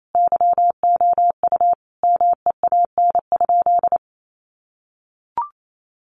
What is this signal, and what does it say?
Morse code, heard as a keyed tone, about 28 wpm, 700 Hz. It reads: YOU MEAN?
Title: Sentences from Top 900 Words 28wpm